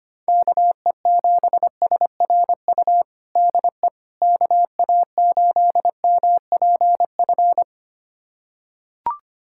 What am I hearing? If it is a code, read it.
Morse code, heard as a keyed tone, about 25 wpm, 700 Hz. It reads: KE7SRU DE KA8MPF
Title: USA Callsign DE Callsign 25wpm